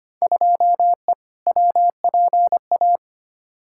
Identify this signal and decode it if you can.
Morse code, heard as a keyed tone, about 25 wpm, 700 Hz. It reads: 2E WPA
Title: Field Day Exchanges - 25wpm